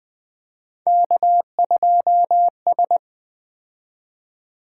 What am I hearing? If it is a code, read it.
Morse code, heard as a keyed tone, about 20 wpm, 700 Hz. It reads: K2S
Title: Three Letter-Number Combinations - 20wpm